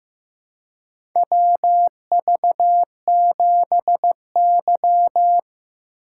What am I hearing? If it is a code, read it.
Morse code, heard as a keyed tone, about 15 wpm, 700 Hz. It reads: WV7Y
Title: USA Call Signs - 4 Characters - 4x Character Spacing - 15wpm